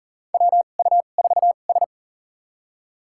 Morse code, heard as a keyed tone, about 40 wpm, spaced as for 29 wpm, 700 Hz. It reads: WU4S
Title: USA Call Signs - 4 Characters - 2x Character Spacing - 40wpm